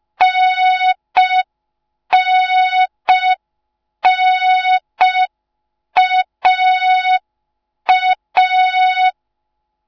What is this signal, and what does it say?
Morse code, heard as a keyed tone, about 5 wpm, 3700 Hz. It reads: NNNAA